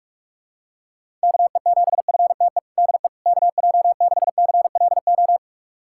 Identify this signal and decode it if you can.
Morse code, heard as a keyed tone, about 45 wpm, 700 Hz. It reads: KE7FTE DE KJ6CLO
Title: USA Callsign DE Callsign - 6 Characters - 45wpm